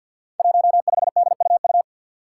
Morse code, heard as a keyed tone, about 52 wpm, 700 Hz. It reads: 15DUU
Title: Five Challenging Letter-Number Combinations 50wpm